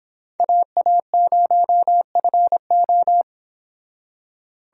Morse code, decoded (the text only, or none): AA0FO